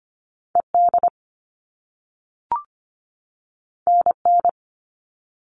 Morse code, heard as a keyed tone, about 25 wpm, 700 Hz. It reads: ED NN
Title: All Cut Numbers - 2-Digit 25wpm